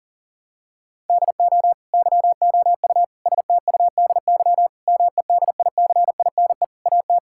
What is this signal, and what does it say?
Morse code, heard as a keyed tone, about 40 wpm, 700 Hz. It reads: DO YOU STUDY MEDICINE AT
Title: Sentences from Top 1000 Words - Review Part 6 - 40wpm